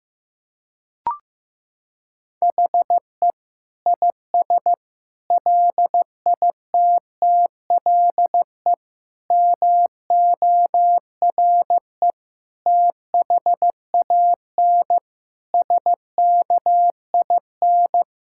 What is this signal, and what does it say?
Morse code, heard as a keyed tone, about 16 wpm, 700 Hz. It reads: HE IS LITTLE MORE THAN SKIN
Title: Sentences from Top 1600 Words 15wpm